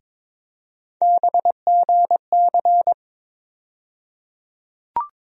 Morse code, heard as a keyed tone, about 21 wpm, 700 Hz. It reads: BGC
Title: USA Call Sign Postfixes - 3 Letters 22wpm